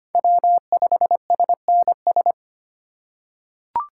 Morse code, heard as a keyed tone, about 25 wpm, 700 Hz. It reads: W5SNS
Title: USA Call Signs - 8x Character Spacing - 25wpm